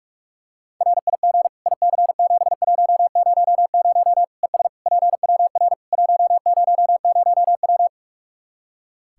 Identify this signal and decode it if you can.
Morse code, heard as a keyed tone, about 47 wpm, 700 Hz. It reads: RIG IC7100 ES PWR 100W
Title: Practice QSO - Exchange 2 - Info - 45wpm